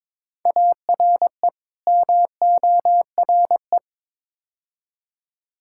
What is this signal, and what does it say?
Morse code, heard as a keyed tone, about 22 wpm, 700 Hz. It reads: ARE MORE
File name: practice / 22wpm / Build A Phrase - 5-Word Phrases - 22wpm